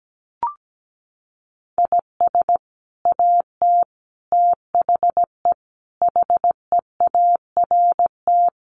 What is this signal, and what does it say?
Morse code, heard as a keyed tone, about 17 wpm, 700 Hz. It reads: IS AT THE HEART
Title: Sets of 5 Words Encore2 17wpm